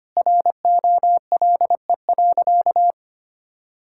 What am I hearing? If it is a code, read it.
Morse code, heard as a keyed tone, about 25 wpm, 700 Hz. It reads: ROLE.
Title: Sentences from Top 1000 Words - Review Part 6 - 25wpm